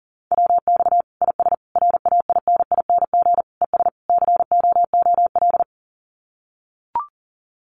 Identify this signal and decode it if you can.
Morse code, heard as a keyed tone, about 40 wpm, 700 Hz. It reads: WX IS RAINING ES COOL
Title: Common QSO Phrases 40wpm